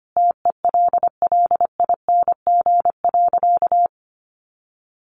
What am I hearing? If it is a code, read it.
Morse code, heard as a keyed tone, about 25 wpm, 700 Hz. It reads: TELLING.
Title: A Sound Of Thunder 25wpm